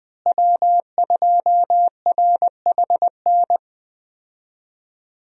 Morse code, heard as a keyed tone, about 20 wpm, 700 Hz. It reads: W2RHN
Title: USA Call Signs - 5 Characters - Repeated 2x - 20wpm